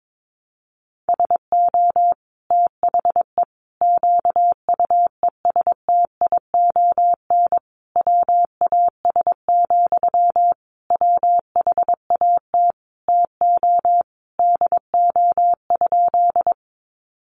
Morse code, heard as a keyed tone, about 22 wpm, 700 Hz. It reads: SO THE QUESTION WAS, WHAT TO DO?
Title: Sentences from Top 400 Words 22wpm